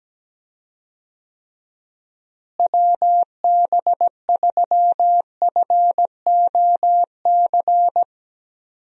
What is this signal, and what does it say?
Morse code, heard as a keyed tone, about 17 wpm, 700 Hz. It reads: WB3FOC
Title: USA Call Signs - 6 Characters - Repeated 2x - 17wpm